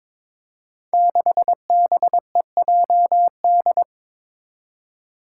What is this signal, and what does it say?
Morse code, heard as a keyed tone, about 22 wpm, 700 Hz. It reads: 6BEJD
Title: Five Challenging Letter-Number Combinations 22wpm